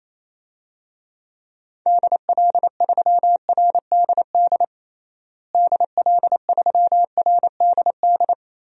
Morse code, heard as a keyed tone, about 27 wpm, 700 Hz. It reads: DL3RDD DL3RDD
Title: DXCC-100 Call Signs - 6 Characters - Repeated 3x - 28wpm